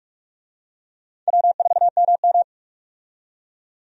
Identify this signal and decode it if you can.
Morse code, heard as a keyed tone, about 46 wpm, 700 Hz. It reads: W4MM
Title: USA Call Signs - 4 Characters - 45wpm